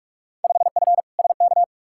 Morse code, heard as a keyed tone, about 46 wpm, 700 Hz. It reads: 5F SK
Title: Field Day Exchanges Encore - 45wpm